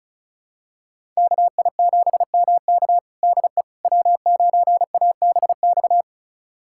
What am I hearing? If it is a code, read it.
Morse code, heard as a keyed tone, about 34 wpm, 690 Hz. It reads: KI7MK DE W9ABX
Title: USA Callsign DE Callsign - 5 Characters - 35wpm